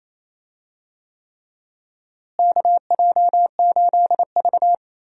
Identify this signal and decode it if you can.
Morse code, heard as a keyed tone, about 28 wpm, 700 Hz. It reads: KJ8V
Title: USA Call Signs - 4 Characters - Repeated 3x - 28wpm